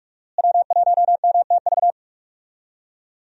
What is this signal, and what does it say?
Morse code, heard as a keyed tone, about 46 wpm, 700 Hz. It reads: W1MTV
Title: USA Call Signs - 5 Characters - Repeated 3x - 45wpm